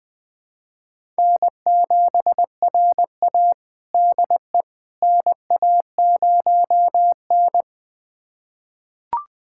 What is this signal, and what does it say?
Morse code, heard as a keyed tone, about 20 wpm, 700 Hz. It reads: N7RA DE NA0N
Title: USA Callsign DE Callsign - 4 Characters - 20wpm